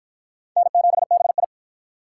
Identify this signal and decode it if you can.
Morse code, heard as a keyed tone, about 48 wpm, 700 Hz. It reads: N7BI